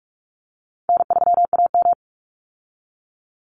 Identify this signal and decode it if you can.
Morse code, heard as a keyed tone, about 46 wpm, 700 Hz. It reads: N3AM
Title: Call Signs - 45wpm